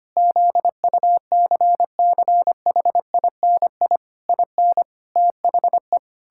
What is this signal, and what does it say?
Morse code, heard as a keyed tone, about 25 wpm, 700 Hz. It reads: ZUCCHINI IN THE
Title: Pangram Challenge v2 - 25wpm